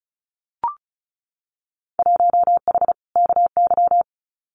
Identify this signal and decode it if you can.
Morse code, heard as a keyed tone, about 34 wpm, 700 Hz. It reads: 1H KY
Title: Winter Field Day Exchanges 35wpm